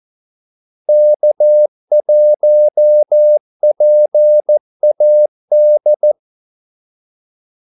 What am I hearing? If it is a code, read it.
Morse code, heard as a keyed tone, about 14 wpm, 600 Hz. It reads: K1PAD
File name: Callsigns-14wpm